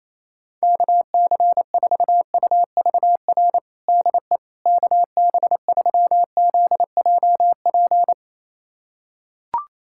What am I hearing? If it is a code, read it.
Morse code, heard as a keyed tone, about 28 wpm, 700 Hz. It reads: KC4UVR DE KB3ZJP
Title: USA Callsign DE Callsign 28wpm